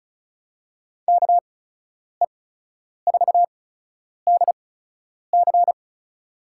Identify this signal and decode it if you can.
Morse code, heard as a keyed tone, about 36 wpm, 700 Hz. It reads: K E 4 D C
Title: USA Call Signs - 6 Characters - 8x Character Spacing - 35wpm